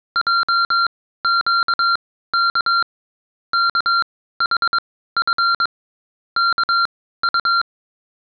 Morse code, heard as a keyed tone, about 22 wpm, spaced as for 15 wpm, 1380 Hz. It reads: JQK KHF KU